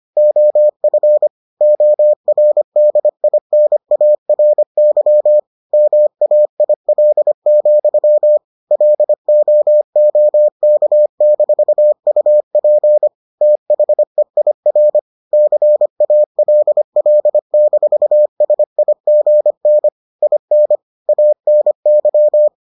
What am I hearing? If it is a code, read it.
Morse code, heard as a keyed tone, about 25 wpm, 600 Hz. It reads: OF ORDINARY MAIL, LOOK-UP THEIR CALL-SIGN IN ANY